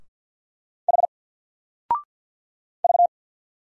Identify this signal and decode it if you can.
Morse code, heard as a keyed tone, about 48 wpm, 700 Hz. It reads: H V
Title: Single Letter-Number 50wpm